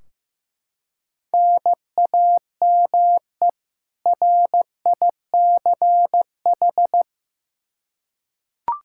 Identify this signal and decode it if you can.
Morse code, heard as a keyed tone, about 15 wpm, 700 Hz. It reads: NAME RICH
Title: K1USN SST QSO Practice 15wpm